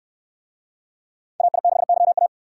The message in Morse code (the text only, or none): AI6YA